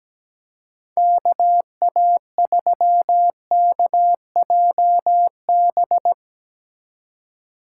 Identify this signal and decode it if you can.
Morse code, heard as a keyed tone, about 19 wpm, 700 Hz. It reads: KA3KJB
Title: Speed Racing - USA Call Signs 17wpm